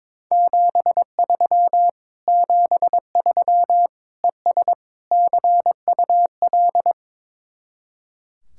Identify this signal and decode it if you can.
Morse code, heard as a keyed tone, about 22 wpm, 700 Hz. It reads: 73 73 ES CUL
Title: Speed-Racing Common QSO Phrases - 22wpm